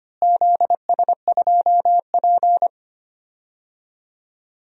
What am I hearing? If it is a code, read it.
Morse code, heard as a keyed tone, about 25 wpm, 700 Hz. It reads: ZS2P
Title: DXCC-100 Call Signs - 4 Characters - Repeated 2x - 25wpm